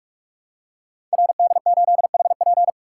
Morse code, heard as a keyed tone, about 44 wpm, 700 Hz. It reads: RD8HP
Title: Code Groups - 5 Letter:Numbers - 45wpm